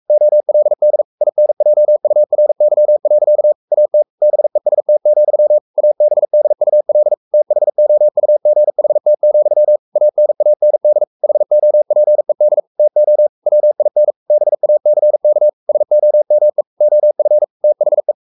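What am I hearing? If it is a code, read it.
Morse code, heard as a keyed tone, about 44 wpm, 600 Hz. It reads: OLD INJURY. AT BEST, ABDUL THOUGHT, ANAND HOPED TO WIN BACK SOME OF THE
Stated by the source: Morse code at 43 wpm sending Ken Follett's novel Never in English, from CW med Gnister.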